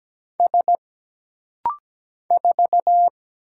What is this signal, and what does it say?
Morse code, heard as a keyed tone, about 17 wpm, 700 Hz. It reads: S 4
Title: Single Letter-Number Alt - Rapid-Fire 17wpm